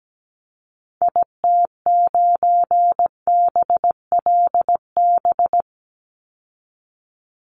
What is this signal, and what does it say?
Morse code, heard as a keyed tone, about 17 wpm, 700 Hz. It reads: IT9BLB
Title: Call Signs Repeated 3x - 17wpm